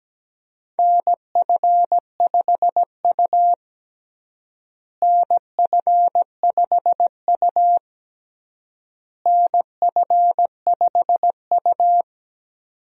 Morse code, heard as a keyed tone, about 17 wpm, 700 Hz. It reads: NF5U NF5U NF5U
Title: USA Call Signs - 4 Characters - Repeated 3x - 17wpm